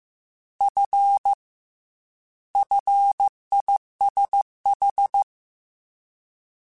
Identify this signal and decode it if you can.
Morse code, heard as a keyed tone, about 15 wpm, 800 Hz. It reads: F FISH
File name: Fs / Single F mp3s / Single F Prefixes 3x2 Twice @13